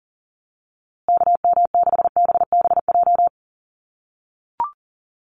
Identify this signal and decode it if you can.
Morse code, heard as a keyed tone, about 40 wpm, 700 Hz. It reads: KM6BBJ
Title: USA Call Signs Repeated 3x 40wpm